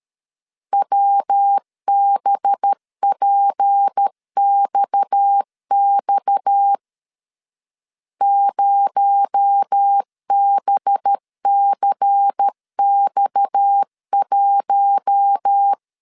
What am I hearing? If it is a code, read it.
Morse code, heard as a keyed tone, about 13 wpm, 800 Hz. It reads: WBPXX 0BCX1
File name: CW / DET / 1 6 1500C S B 12wpm 3431